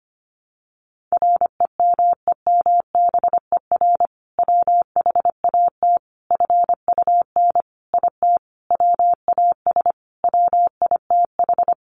Morse code, heard as a keyed tone, about 25 wpm, 700 Hz. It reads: REMEMBER WHAT FUN IT WAS WITH
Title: Sentences from Top 700 Words 25wpm